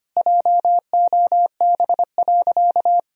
Morse code, heard as a keyed tone, about 25 wpm, 700 Hz. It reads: JOB.
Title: Sentences from Top 1000 Words - Review Part 3 - 25wpm